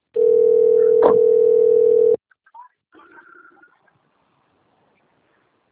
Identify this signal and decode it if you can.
Ringback tone (North America), 440 + 480 Hz.